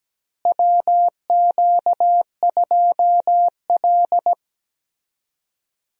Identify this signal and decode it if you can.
Morse code, heard as a keyed tone, about 17 wpm, 700 Hz. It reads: WQ2L
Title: USA Call Signs - 4 Characters - Repeated 2x - 17wpm